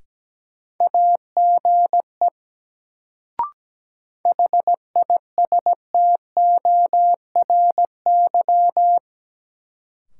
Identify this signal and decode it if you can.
Morse code, heard as a keyed tone, about 17 wpm, 700 Hz. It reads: AGE HISTORY
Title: Sets of 3 Words - New Word Drill - 17wpm